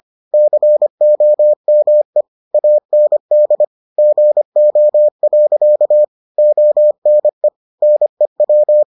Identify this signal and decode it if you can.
Morse code, heard as a keyed tone, about 25 wpm, 600 Hz. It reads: COME AND GO. ONE NEW